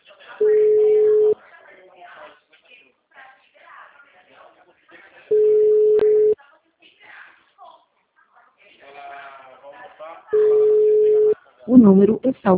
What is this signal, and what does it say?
Ringback tone (Europe, 425 Hz), with speech.